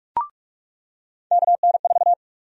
Morse code, heard as a keyed tone, about 46 wpm, 700 Hz. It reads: KN4
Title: USA Call Sign Prefixes - 45wpm